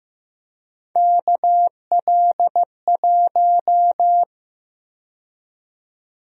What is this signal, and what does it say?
Morse code, heard as a keyed tone, about 15 wpm, 700 Hz. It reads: KL1
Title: USA Call Sign Prefixes - 15wpm